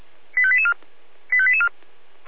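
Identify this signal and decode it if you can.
Five-tone selective call (ZVEI), heard as quick runs of five tones, tones of about 70 ms. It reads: 85884 85884
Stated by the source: ZVEI five-tone selective call 85869 received by SDR.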